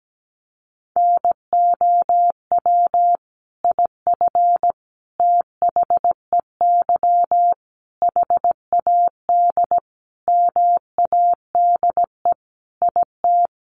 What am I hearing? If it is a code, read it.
Morse code, heard as a keyed tone, about 17 wpm, 700 Hz. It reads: NOW IF THEY HAD MADE IT
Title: Sentences from Top 500 Words 17wpm